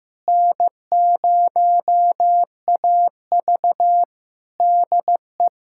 Morse code, heard as a keyed tone, about 15 wpm, 700 Hz. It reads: N0AV DE
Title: Practice QSO - Exchange 4 - Ending - 15wpm